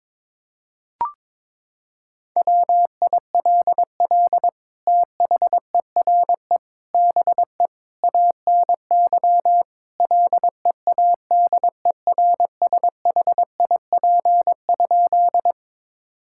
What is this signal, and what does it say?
Morse code, heard as a keyed tone, about 21 wpm, 700 Hz. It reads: WILL THERE BE ANY LEADERSHIP?
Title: Sentences from Top 1600 Words 22wpm